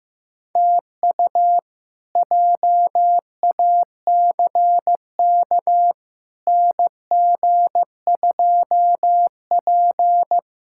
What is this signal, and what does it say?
Morse code, heard as a keyed tone, about 15 wpm, 700 Hz. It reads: TU JACK NG2P
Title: K1USN SST QSO Practice - 15wpm